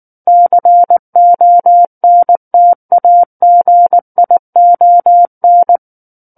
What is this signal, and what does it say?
Morse code, heard as a keyed tone, about 19 wpm, 700 Hz. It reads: CONTAGION
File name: Book One / 10-Chapter 10 - 19 WPM